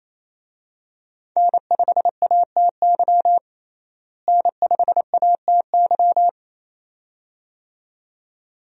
Morse code, heard as a keyed tone, about 28 wpm, 700 Hz. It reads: N5ATY N5ATY
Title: USA Call Signs - 5 Characters - Repeated 2x - 28wpm